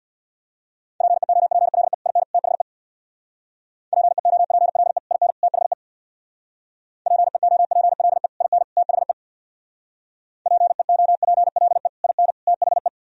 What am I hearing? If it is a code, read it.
Morse code, heard as a keyed tone, about 64 wpm, 700 Hz. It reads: PEOPLE IN THE PEOPLE IN THE PEOPLE IN THE PEOPLE IN THE